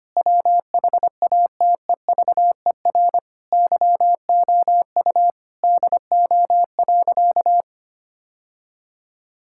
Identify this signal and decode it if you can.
Morse code, heard as a keyed tone, about 25 wpm, 700 Hz. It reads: WHATEVER YOU DO.
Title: Sentences from Top 1600 Words 25wpm